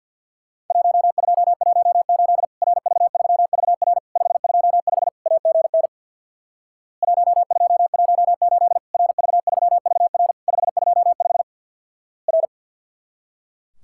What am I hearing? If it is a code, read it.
Morse code, heard as a keyed tone, about 50 wpm, 700 Hz. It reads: 1218 RV3VR 525 AGN 1218 RV3VR 525 R
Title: Worked All Europe DX Contest - QTC Exchange with Repeats - 50wpm